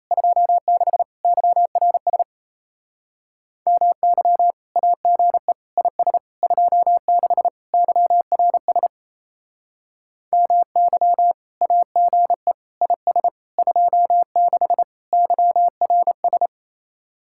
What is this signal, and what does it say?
Morse code, heard as a keyed tone, about 34 wpm, 700 Hz. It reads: 26 YRS MY AGE IS 26 YRS MY AGE IS 26 YRS
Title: Speed-Racing Common QSO Phrases - 28wpm